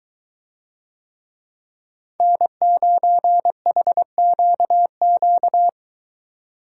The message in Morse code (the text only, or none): N9HQQ